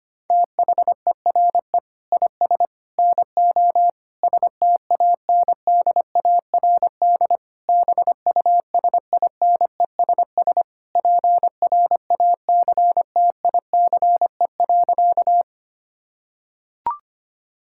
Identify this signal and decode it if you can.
Morse code, heard as a keyed tone, about 25 wpm, 700 Hz. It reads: THERE IS NO STANDARD BUSINESS PRACTICE.
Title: Sentences from Top 1000 Words - Review Part 6 - 25wpm